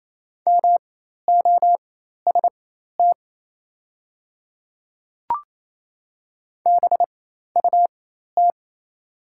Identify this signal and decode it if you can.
Morse code, heard as a keyed tone, about 28 wpm, spaced as for 13 wpm, 700 Hz. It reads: MOST BUT
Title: 4x Character Spacing - Top 200 Words - 28wpm